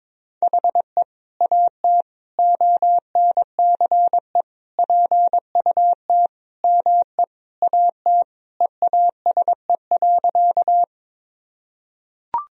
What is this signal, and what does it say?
Morse code, heard as a keyed tone, about 22 wpm, 700 Hz. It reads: HE AT ONCE PUT ME AT EASE.
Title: Sentences from Top 1000 Words - Review Part 1 - 22wpm